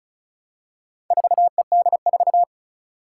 Morse code, heard as a keyed tone, about 36 wpm, 690 Hz. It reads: 4ED4
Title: Four Challenging Letter-Number Combinations 35wpm